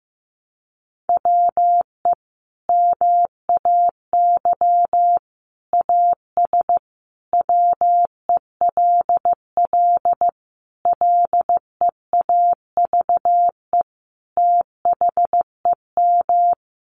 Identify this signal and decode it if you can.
Morse code, heard as a keyed tone, about 14 wpm, 700 Hz. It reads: WE MAY AS WELL LEAVE THEM
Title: Sentences from Top 800 Words 15wpm